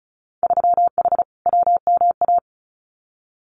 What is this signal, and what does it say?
Morse code, heard as a keyed tone, about 36 wpm, 700 Hz. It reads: 3H WMA
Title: Winter Field Day Exchanges Encore 35wpm